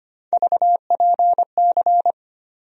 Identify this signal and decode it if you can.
Morse code, heard as a keyed tone, about 25 wpm, 700 Hz. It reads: VPC
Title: USA Call Sign Postfixes - 3 Letters 25wpm